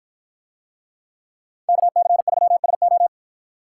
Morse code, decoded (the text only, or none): KC3SO